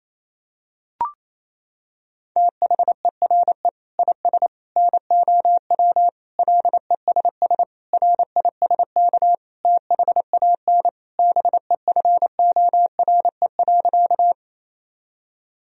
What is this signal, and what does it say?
Morse code, heard as a keyed tone, about 27 wpm, 700 Hz. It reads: THERE IS NOW LESS RISK THAN BEFORE.
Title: Sentences from Top 1000 Words - Review Part 4 - 28wpm